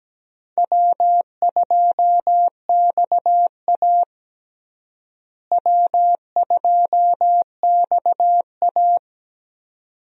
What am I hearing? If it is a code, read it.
Morse code, heard as a keyed tone, about 17 wpm, 700 Hz. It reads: W2XA W2XA
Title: USA Call Signs - 4 Characters - Repeated 2x - 17wpm